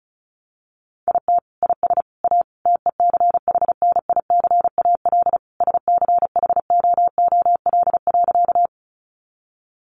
Morse code, heard as a keyed tone, about 34 wpm, 700 Hz. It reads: IT IS A TECHNICAL SCHOOL.
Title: Sentences from Top 1100 Words 35wpm